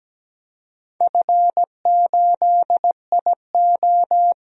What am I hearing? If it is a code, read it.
Morse code, heard as a keyed tone, about 17 wpm, 700 Hz. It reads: F8IO
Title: DXCC-100 Call Signs - 4 Characters - Repeated 3x - 17wpm